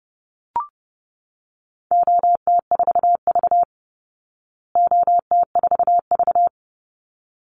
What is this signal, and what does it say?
Morse code, heard as a keyed tone, about 30 wpm, 700 Hz. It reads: OT4V OT4V
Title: Call Signs Repeated 3x - 30wpm